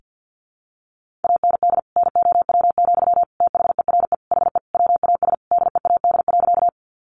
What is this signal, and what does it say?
Morse code, heard as a keyed tone, about 50 wpm, 700 Hz. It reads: AND NOW, THERE HE WAS DEAD.
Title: Sentences from Top 1500 Words 50wpm